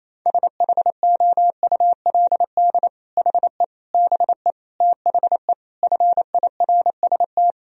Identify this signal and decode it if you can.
Morse code, heard as a keyed tone, about 27 wpm, 700 Hz. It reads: SHOULD HE BE THE FIRST
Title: Sentences from Top 1000 Words - Review Part 4 - 28wpm